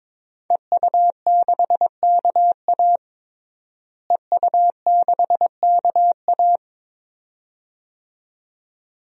Morse code, heard as a keyed tone, about 22 wpm, 700 Hz. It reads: EU6KA EU6KA
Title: DXCC-100 Call Signs - 5 Characters - Repeated 2x - 22wpm